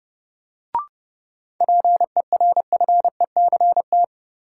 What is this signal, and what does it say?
Morse code, heard as a keyed tone, about 31 wpm, 700 Hz. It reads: PERFECT
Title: Sentences from Top 1000 Words - New Word Drill - 30wpm